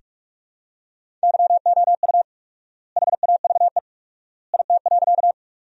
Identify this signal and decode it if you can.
Morse code, heard as a keyed tone, about 43 wpm, 700 Hz. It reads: YOU HAVE IT.
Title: 4x Word-Spacing - Sentences from Top 100 Words - 45wpm